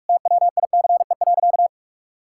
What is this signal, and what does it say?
Morse code, heard as a keyed tone, about 46 wpm, 700 Hz. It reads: TWICE.
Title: Sentences from Top 1600 Words 45wpm